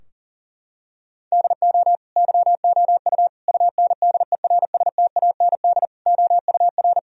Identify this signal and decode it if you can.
Morse code, heard as a keyed tone, about 40 wpm, 700 Hz. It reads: DO YOU UNDERSTAND OUR
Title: Sentences from Top 1000 Words - Review Part 3 - 40wpm